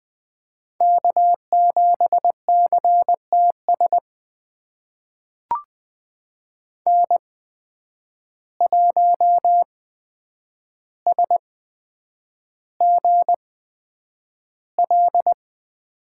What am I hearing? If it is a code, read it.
Morse code, heard as a keyed tone, about 20 wpm, spaced as for 9 wpm, 700 Hz. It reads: K7CTS N1SGL
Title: USA Call Signs - 5 Characters - 8x Character Spacing - 20wpm